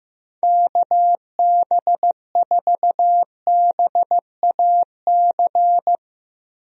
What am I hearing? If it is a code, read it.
Morse code, heard as a keyed tone, about 15 wpm, 700 Hz. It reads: KB4BAC
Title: USA Call Signs - 6 Characters - 15wpm